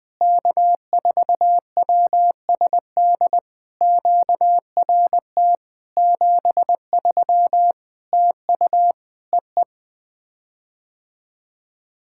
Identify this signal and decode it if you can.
Morse code, heard as a keyed tone, about 20 wpm, 700 Hz. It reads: K4WSD QRT 73 TU EE
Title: POTA QSO Practice - 20wpm